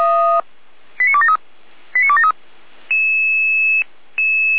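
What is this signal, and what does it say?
Five-tone selective call (ZVEI), heard as quick runs of five tones, tones of about 70 ms. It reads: 89273 89273
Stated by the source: ZVEI five-tone selective call 89126 received on an SDR.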